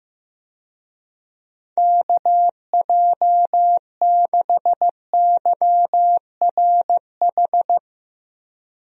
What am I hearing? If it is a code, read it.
Morse code, heard as a keyed tone, about 15 wpm, 700 Hz. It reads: KJ6YRH
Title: USA Call Signs - 6 Characters - 15wpm